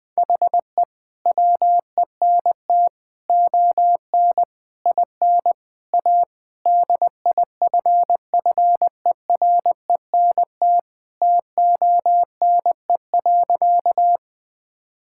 Morse code, heard as a keyed tone, about 20 wpm, 700 Hz. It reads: HE WENT ON IN A DIFFERENT TONE.
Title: Sentences from Top 1600 Words 20wpm